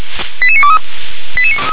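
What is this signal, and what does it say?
Five-tone selective call (ZVEI), heard as quick runs of five tones, tones of about 70 ms. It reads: 88923
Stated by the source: ZVEI five-tone selective call 88954 received by SDR.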